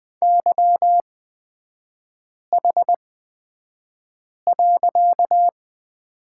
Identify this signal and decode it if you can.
Morse code, heard as a keyed tone, about 20 wpm, spaced as for 5 wpm, 700 Hz. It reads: YH.
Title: 255 Lesson 41e - Period - Three Characters 20wpm